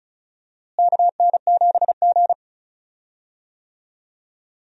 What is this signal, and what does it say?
Morse code, heard as a keyed tone, about 36 wpm, 700 Hz. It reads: KN7G
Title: USA Call Signs - 4 Characters - Repeated 2x - 35wpm